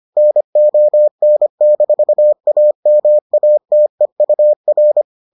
Morse code, heard as a keyed tone, about 25 wpm, 600 Hz. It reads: NON-AMATEUR